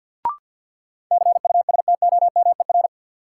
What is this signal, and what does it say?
Morse code, heard as a keyed tone, about 52 wpm, 700 Hz. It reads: CUSTOMER
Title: Sentences from Top 1100 Words - New Word Drill Alt - 50wpm